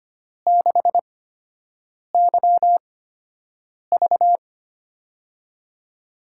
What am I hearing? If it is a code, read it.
Morse code, heard as a keyed tone, about 25 wpm, spaced as for 7 wpm, 700 Hz. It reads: 6YV
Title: USA Call Signs - 8x Character Spacing - 25wpm